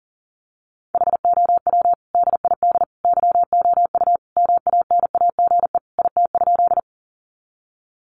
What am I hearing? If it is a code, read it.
Morse code, heard as a keyed tone, about 40 wpm, 700 Hz. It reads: HOW DID YOU MANAGE IT?
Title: Sentences from Top 1400 Words 40wpm